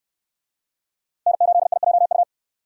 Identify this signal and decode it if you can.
Morse code, heard as a keyed tone, about 68 wpm, 700 Hz. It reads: N8IJU